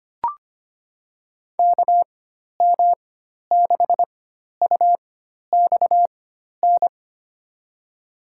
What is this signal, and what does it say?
Morse code, heard as a keyed tone, about 25 wpm, spaced as for 12 wpm, 700 Hz. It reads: KM6UXN
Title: USA Call Signs - 6 Characters - 4x Character Spacing - 25wpm